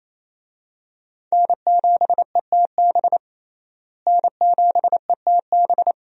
Morse code, heard as a keyed tone, about 28 wpm, 700 Hz. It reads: N7ETB N7ETB
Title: USA Call Signs - 5 Characters - Repeated 2x - 28wpm